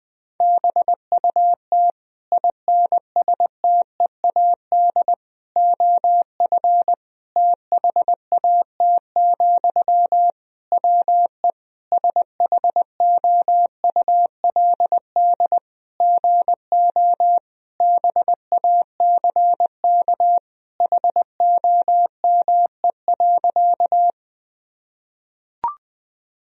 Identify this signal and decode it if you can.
Morse code, heard as a keyed tone, about 20 wpm, 700 Hz. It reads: BUT INSTEAD OF THAT, WE SHOULD GO BACK HOME.
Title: Sentences from Top 1000 Words - Review Part 4 - 20wpm